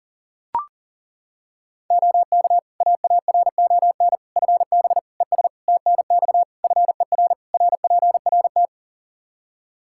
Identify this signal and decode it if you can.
Morse code, heard as a keyed tone, about 40 wpm, 700 Hz. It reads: OK AARON FB ES TNX FER RPRT
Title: Practice QSO - Exchange 2 - Info - 40wpm